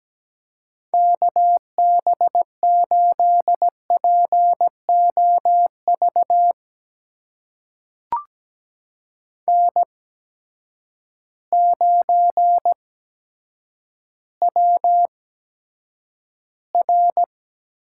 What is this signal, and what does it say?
Morse code, heard as a keyed tone, about 17 wpm, spaced as for 8 wpm, 700 Hz. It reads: KB8POV N9WR
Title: USA Call Signs - 8x Character Spacing - 17wpm